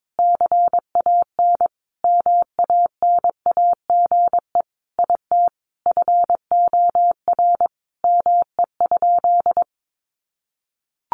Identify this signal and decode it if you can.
Morse code, heard as a keyed tone, about 22 wpm, 700 Hz. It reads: CAN MANAGE IT FOR ME?
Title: Sentences from Top 1400 Words 22wpm